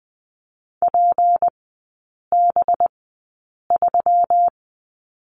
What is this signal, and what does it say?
Morse code, heard as a keyed tone, about 20 wpm, spaced as for 8 wpm, 700 Hz. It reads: PB3
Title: 184 Lesson 30e - Letter b - Three Characters 20wpm